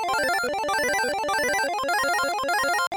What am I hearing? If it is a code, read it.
There is no code.